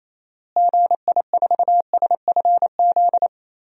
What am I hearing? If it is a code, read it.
Morse code, heard as a keyed tone, about 28 wpm, 700 Hz. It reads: GI4SFZ
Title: DXCC-100 Call Signs - 6 Characters - Repeated 2x - 28wpm